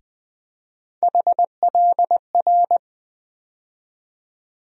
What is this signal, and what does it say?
Morse code, heard as a keyed tone, about 20 wpm, 700 Hz. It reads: HLR
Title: USA Call Sign Postfixes - 3 Letters 20wpm